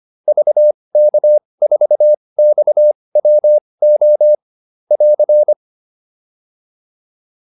Morse code, heard as a keyed tone, about 25 wpm, spaced as for 20 wpm, 600 Hz. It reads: VK4XWO <AR>
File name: QSO407 20